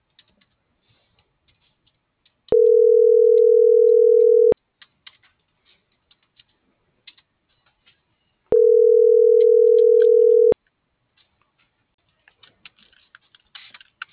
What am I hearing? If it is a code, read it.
Ringback tone (North America), 440 + 480 Hz.